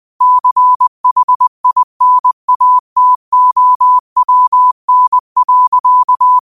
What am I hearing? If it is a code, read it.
Morse code, heard as a keyed tone, about 20 wpm, 1000 Hz. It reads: CHINATOWN.
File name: qotdfiles / 20wpm